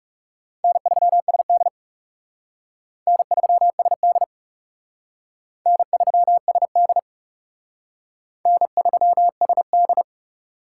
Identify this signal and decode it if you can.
Morse code, heard as a keyed tone, about 37 wpm, 700 Hz. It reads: N3SD N3SD N3SD N3SD
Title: Speed Racing - USA 1x2 Call Signs 30wpm